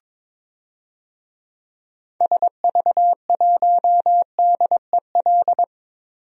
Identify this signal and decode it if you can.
Morse code, heard as a keyed tone, about 22 wpm, 700 Hz. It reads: SV1DEL
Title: DXCC-100 Call Signs - 6 Characters - Repeated 2x - 22wpm